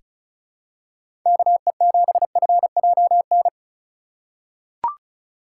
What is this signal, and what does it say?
Morse code, heard as a keyed tone, about 36 wpm, 700 Hz. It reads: KE7FJN
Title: USA Call Signs - 6 Characters - 2x Character Spacing - 35wpm